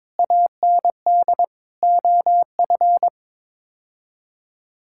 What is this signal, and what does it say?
Morse code, heard as a keyed tone, about 22 wpm, 700 Hz. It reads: AND OF
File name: Sets of 9 Words - 22wpm